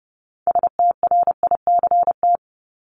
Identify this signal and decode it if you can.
Morse code, heard as a keyed tone, about 30 wpm, 700 Hz. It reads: STRICT
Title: Top 1501-1750 Words 30wpm